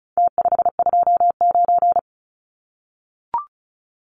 Morse code, heard as a keyed tone, about 34 wpm, 700 Hz. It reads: T529
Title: Numbers - Contest Style - 4-Digit 35wpm